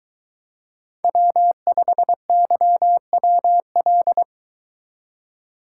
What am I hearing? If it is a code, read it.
Morse code, heard as a keyed tone, about 23 wpm, 700 Hz. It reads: W5YWL